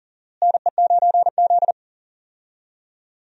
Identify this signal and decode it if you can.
Morse code, heard as a keyed tone, about 40 wpm, 700 Hz. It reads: NE9Z
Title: USA Call Signs - 4 Characters - Repeated 2x - 40wpm